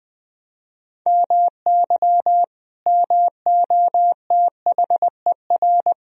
Morse code, heard as a keyed tone, about 20 wpm, 700 Hz. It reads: MY MOTHER
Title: Sentences from Top 1000 Words - Review Part 1 - 20wpm